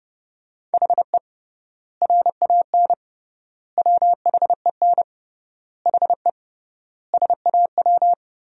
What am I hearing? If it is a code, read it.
Morse code, heard as a keyed tone, about 30 wpm, 700 Hz. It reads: HE RAN WHEN HE SAW
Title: 3x Word-Spacing - Sentences from Top 500 Words - 30wpm